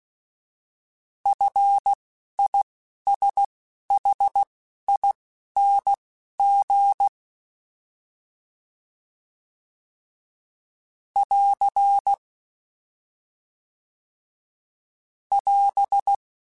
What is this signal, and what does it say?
Morse code, heard as a keyed tone, about 16 wpm, spaced as for 11 wpm, 800 Hz. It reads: FISHING <AR> <AS>
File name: Fs / Single F mp3s / Single F Prefixes 3x2 Twice @10